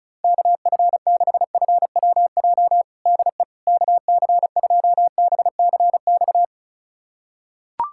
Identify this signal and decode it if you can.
Morse code, heard as a keyed tone, about 36 wpm, 700 Hz. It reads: KF6FWJ DE KC2BCX
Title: USA Callsign DE Callsign - 6 Characters - 35wpm